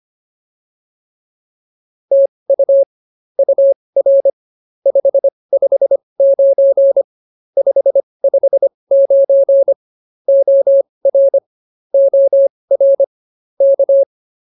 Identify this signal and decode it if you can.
Morse code, heard as a keyed tone, about 24 wpm, spaced as for 20 wpm, 550 Hz. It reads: TU UR 559 559 OR OR K